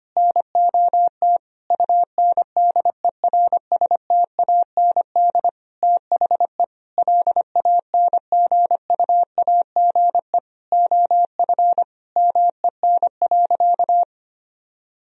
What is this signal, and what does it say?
Morse code, heard as a keyed tone, about 25 wpm, 700 Hz. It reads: NOT UNDERSTAND THE LANGUAGE OF MEN.
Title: Sentences from Top 1000 Words - Review Part 3 - 25wpm